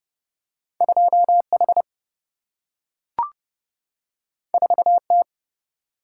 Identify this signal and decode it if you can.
Morse code, heard as a keyed tone, about 30 wpm, 700 Hz. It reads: 2 H 4 T
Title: Single-Letter Single-Number 30wpm